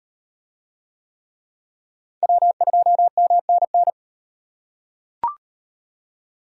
Morse code, heard as a keyed tone, about 38 wpm, 700 Hz. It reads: W2MNN